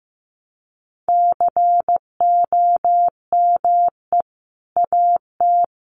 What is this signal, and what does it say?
Morse code, heard as a keyed tone, about 15 wpm, 700 Hz. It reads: COME AT
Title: Sentences from Top 300 Words 15wpm